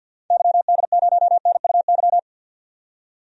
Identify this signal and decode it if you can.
Morse code, heard as a keyed tone, about 50 wpm, 700 Hz. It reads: YD0NUY
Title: DXCC-100 Call Signs - 6 Characters - 50wpm